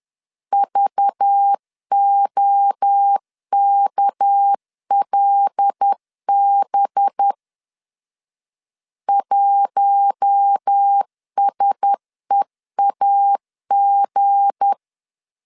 Morse code, heard as a keyed tone, about 10 wpm, 800 Hz. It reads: VOKLB 1SEAG
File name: LEZIONI / 7 GF9Z / 7 G 10wpm 2501